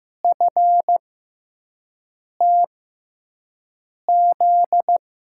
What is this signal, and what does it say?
Morse code, heard as a keyed tone, about 15 wpm, spaced as for 5 wpm, 700 Hz. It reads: FTZ
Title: USA Call Signs - 6x Character Spacing - 15wpm